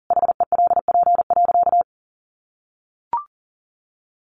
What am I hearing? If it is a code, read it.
Morse code, heard as a keyed tone, about 40 wpm, 700 Hz. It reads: HELP.
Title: Sentences from Top 1100 Words 40wpm